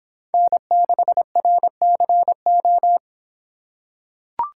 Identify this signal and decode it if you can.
Morse code, heard as a keyed tone, about 26 wpm, 700 Hz. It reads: N6RCO